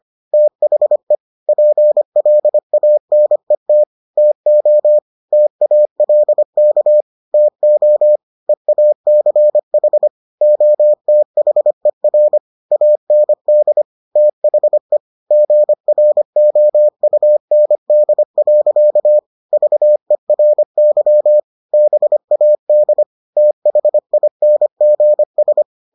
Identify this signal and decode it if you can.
Morse code, heard as a keyed tone, about 25 wpm, 600 Hz. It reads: THE PLANET TO TALK TO EACH OTHER AND THE GROUND. VERY BAD THINGS